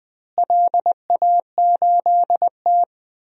Morse code, heard as a keyed tone, about 20 wpm, 700 Hz. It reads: LA8T
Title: DXCC-100 Call Signs - 4 Characters - Repeated 3x - 20wpm